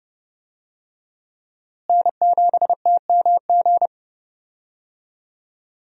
Morse code, heard as a keyed tone, about 30 wpm, 700 Hz. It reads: N7TMG
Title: USA Call Signs - 5 Characters - Repeated 3x - 30wpm